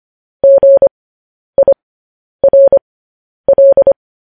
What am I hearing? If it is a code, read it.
Morse code, heard as a keyed tone, about 25 wpm, spaced as for 10 wpm, 550 Hz. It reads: GIRL
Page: news-headlines-in-morse-code-at-10-wpm